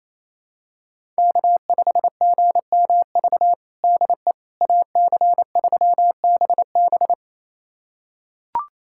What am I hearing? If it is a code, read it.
Morse code, heard as a keyed tone, about 28 wpm, 700 Hz. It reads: K5GMV DE AC3BB
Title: USA Callsign DE Callsign - 5 Characters - 28wpm